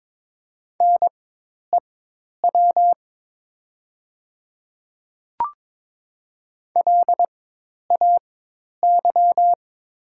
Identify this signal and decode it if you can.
Morse code, heard as a keyed tone, about 22 wpm, spaced as for 10 wpm, 700 Hz. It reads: NEW LAY
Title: 4x Character Spacing - Top 500 Words - 22wpm